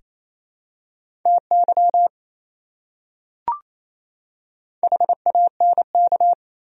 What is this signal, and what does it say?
Morse code, heard as a keyed tone, about 28 wpm, 700 Hz. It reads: TY HANK
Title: USA Common Names - Max 5 Letters - 28wpm